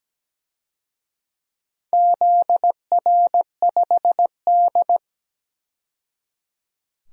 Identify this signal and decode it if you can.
Morse code, heard as a keyed tone, about 17 wpm, 700 Hz. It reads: ZR5D
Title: DXCC-100 Call Signs - 4 Characters - 17wpm